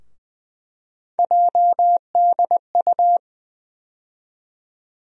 Morse code, heard as a keyed tone, about 20 wpm, 700 Hz. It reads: JDU
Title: USA Call Sign Postfixes - 3 Letters 20wpm